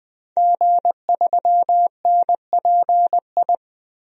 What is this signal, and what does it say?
Morse code, heard as a keyed tone, about 20 wpm, 700 Hz. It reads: G3NPI
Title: DXCC-100 Call Signs - 5 Characters - Repeated 2x - 20wpm